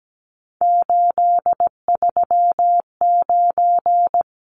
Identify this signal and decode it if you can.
Morse code, heard as a keyed tone, about 17 wpm, 700 Hz. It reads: 839
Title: Numbers - 3-Digit 17wpm